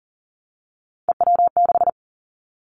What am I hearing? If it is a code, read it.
Morse code, heard as a keyed tone, about 40 wpm, 700 Hz. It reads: EW6
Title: Callsign Prefixes for Top 50 DXCC - 40wpm